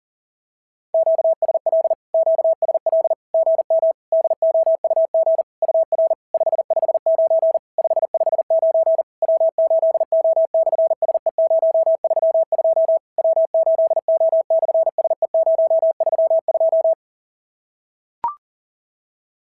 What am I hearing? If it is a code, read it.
Morse code, heard as a keyed tone, about 40 wpm, 690 Hz. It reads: QSL QSL GM DOUG UR 559 559 W8O/SE032 W8O/SE032
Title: SOTA QSO Practice 40wpm